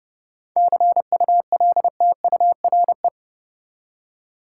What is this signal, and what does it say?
Morse code, heard as a keyed tone, about 30 wpm, 700 Hz. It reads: CULTURE
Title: Seven Letter Words - 30wpm